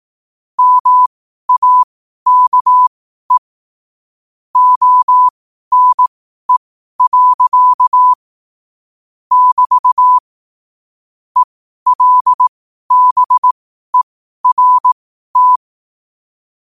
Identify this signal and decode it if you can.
Morse code, heard as a keyed tone, about 18 wpm, spaced as for 12 wpm, 1000 Hz. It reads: MAKE ONE. <BT> ELBERT
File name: qotdfiles / 12wpm